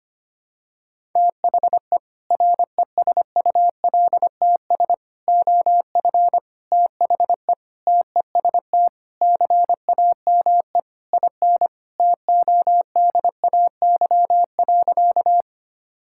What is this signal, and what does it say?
Morse code, heard as a keyed tone, about 25 wpm, 700 Hz. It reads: THE RESULTS OF THE TEST CAME IN TODAY.
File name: practice / 25wpm / Sentences with 4-Word Phrases - 25wpm